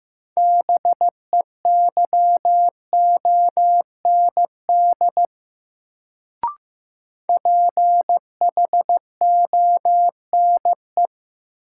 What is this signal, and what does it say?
Morse code, heard as a keyed tone, about 15 wpm, 700 Hz. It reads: BEYOND PHONE
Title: Sentences from Top 1100 Words - New Word Drill Alt - 15wpm